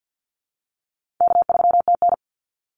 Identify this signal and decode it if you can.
Morse code, heard as a keyed tone, about 49 wpm, 700 Hz. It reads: K3TN
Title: Easy Call Signs Repeated 2x - 50wpm